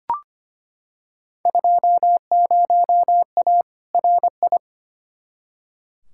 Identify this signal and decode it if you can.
Morse code, heard as a keyed tone, about 25 wpm, 700 Hz. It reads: 20A RI
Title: Field Day Exchanges Encore - 25wpm